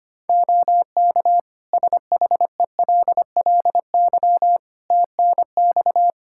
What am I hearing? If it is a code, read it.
Morse code, heard as a keyed tone, about 25 wpm, 700 Hz. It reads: OK SHELLY TNX
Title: Practice QSO - Exchange 4 - Ending - 25wpm